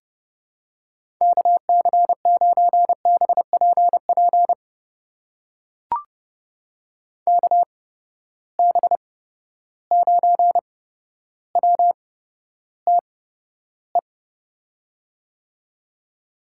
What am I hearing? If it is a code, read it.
Morse code, heard as a keyed tone, about 30 wpm, spaced as for 13 wpm, 700 Hz. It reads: KC9BPP KB9WTE
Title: USA Call Signs - 6 Characters - 8x Character Spacing - 30wpm